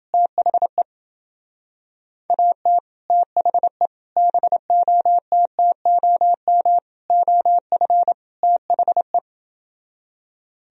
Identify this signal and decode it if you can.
Morse code, heard as a keyed tone, about 28 wpm, 700 Hz. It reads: THE AT THE BOTTOM OF THE